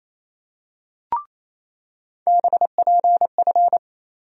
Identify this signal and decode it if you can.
Morse code, heard as a keyed tone, about 27 wpm, 700 Hz. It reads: BPF
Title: USA Call Sign Postfixes - 3 Letters 28wpm